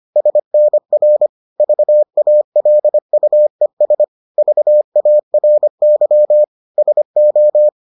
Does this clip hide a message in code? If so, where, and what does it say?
Yes: Morse code from 0.2 to 7.7 s, heard as a keyed tone, about 25 wpm, 600 Hz. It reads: SNR VALUES VARY SO